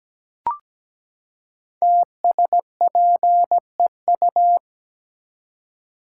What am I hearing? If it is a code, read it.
Morse code, heard as a keyed tone, about 17 wpm, 700 Hz. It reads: TSPEU
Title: Code Groups - 5 Letter:Numbers - 17wpm